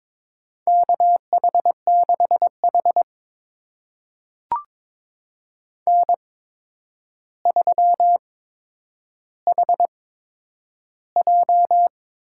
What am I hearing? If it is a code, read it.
Morse code, heard as a keyed tone, about 22 wpm, spaced as for 9 wpm, 700 Hz. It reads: KH6H N3HJ
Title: USA Call Signs - 4 Characters - 8x Character Spacing - 22wpm